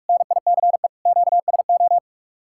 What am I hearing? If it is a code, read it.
Morse code, heard as a keyed tone, about 58 wpm, 700 Hz. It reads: NICE QSO